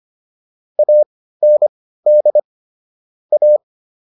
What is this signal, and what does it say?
Morse code, heard as a keyed tone, about 25 wpm, spaced as for 15 wpm, 600 Hz. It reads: AND A